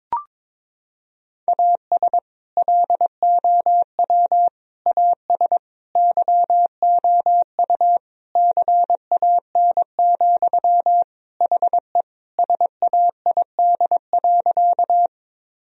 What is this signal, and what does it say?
Morse code, heard as a keyed tone, about 22 wpm, 700 Hz. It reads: AS LOW AS YOU CAN, HE SAID.
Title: Sentences from Top 1000 Words - Review Part 1 - 22wpm